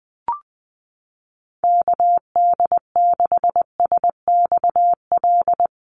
Morse code, heard as a keyed tone, about 20 wpm, 700 Hz. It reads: KD6SXL
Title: USA Call Signs Repeated 2x 20wpm